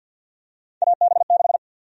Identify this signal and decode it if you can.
Morse code, heard as a keyed tone, about 53 wpm, 700 Hz. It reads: AB6
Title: USA Call Sign Prefixes - 50wpm